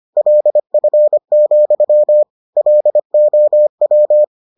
Morse code, heard as a keyed tone, about 25 wpm, 600 Hz. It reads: LF, LOW